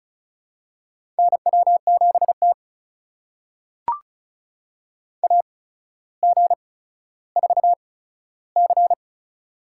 Morse code, heard as a keyed tone, about 36 wpm, spaced as for 15 wpm, 700 Hz. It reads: NW7T AG4C
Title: USA Call Signs - 4 Characters - 8x Character Spacing - 35wpm